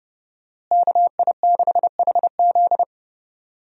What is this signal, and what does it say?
Morse code, heard as a keyed tone, about 32 wpm, 700 Hz. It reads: KI6HZ
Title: USA Call Signs - 5 Characters - Repeated 3x - 30wpm